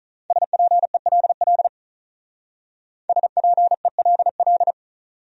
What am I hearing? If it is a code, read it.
Morse code, heard as a keyed tone, about 45 wpm, 700 Hz. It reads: SPELL SPELL